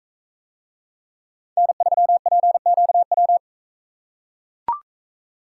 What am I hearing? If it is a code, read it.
Morse code, heard as a keyed tone, about 43 wpm, 700 Hz. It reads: N3PQW